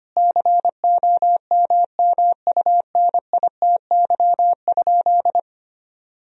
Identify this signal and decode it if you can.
Morse code, heard as a keyed tone, about 25 wpm, 700 Hz. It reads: COMMUNITY?
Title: Sentences from Top 1000 Words - Review Part 4 - 25wpm